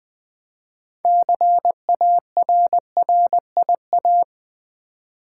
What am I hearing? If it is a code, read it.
Morse code, heard as a keyed tone, about 20 wpm, 700 Hz. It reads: CARRIA
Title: Build A Word - 2 Letters At A Time - 8-Letter Words - 20wpm